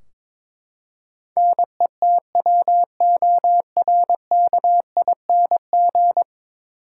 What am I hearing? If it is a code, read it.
Morse code, heard as a keyed tone, about 22 wpm, 700 Hz. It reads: NETWORKING
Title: Practice QSO - Exchange 3 - More Info - 22wpm